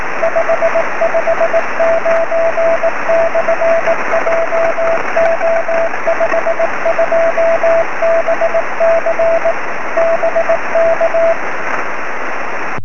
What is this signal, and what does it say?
Morse code, heard as a keyed tone, about 18 wpm, 660 Hz, with radio noise. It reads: 559/JO52BC BK
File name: sm7lcb / 20 23cm / 1558 dl4ol 23cm